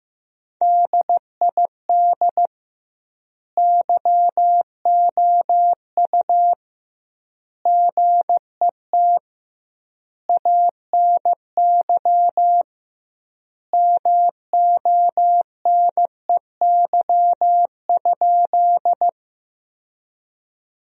Morse code, heard as a keyed tone, about 15 wpm, 700 Hz. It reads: DID YOU GET ANY MONEY?
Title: 2x Word-Spacing - Sentences from Top 400 Words - 15wpm